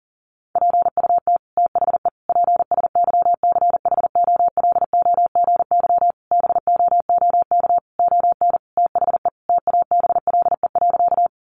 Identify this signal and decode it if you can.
Morse code, heard as a keyed tone, about 37 wpm, 700 Hz. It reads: PUT THE PSYCHOLOGY BOOK ON THE TABLE.
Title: Sentences from Top 1400 Words 40wpm